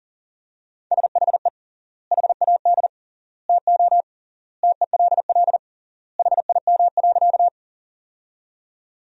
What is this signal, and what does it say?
Morse code, heard as a keyed tone, about 40 wpm, 700 Hz. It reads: SHE HAD TO TELL HIM.
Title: 3x Word-Spacing - Sentences from Top 200 Words - 40wpm